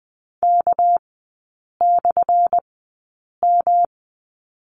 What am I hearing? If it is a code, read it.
Morse code, heard as a keyed tone, about 20 wpm, spaced as for 8 wpm, 690 Hz. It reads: K/M
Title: 212 Lesson 34e - Number 8 - Three Characters 20wpm